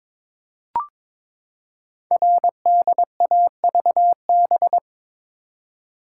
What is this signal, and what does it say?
Morse code, heard as a keyed tone, about 22 wpm, 700 Hz. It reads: RDAVB
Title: Code Groups - 5 Letters - 22wpm